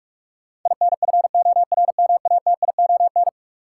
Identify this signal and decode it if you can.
Morse code, heard as a keyed tone, about 44 wpm, 700 Hz. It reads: INFORMATION
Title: Sentences from Top 600 Words - New Word Drill - 45wpm